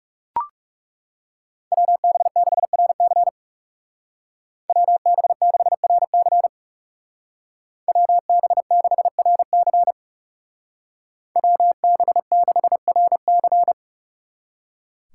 Morse code, heard as a keyed tone, about 39 wpm, 700 Hz. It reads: WB6RC WB6RC WB6RC WB6RC